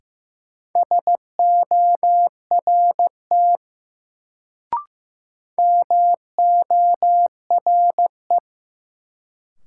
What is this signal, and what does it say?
Morse code, heard as a keyed tone, about 15 wpm, 700 Hz. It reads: SORT MORE
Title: Sets of 2 Words - New Word Drill - 15wpm